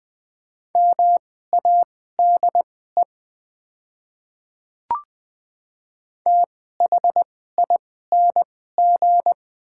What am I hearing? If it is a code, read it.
Morse code, heard as a keyed tone, about 20 wpm, spaced as for 14 wpm, 700 Hz. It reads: MADE THING
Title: 2x Character Spacing - Top 300 Words - 20wpm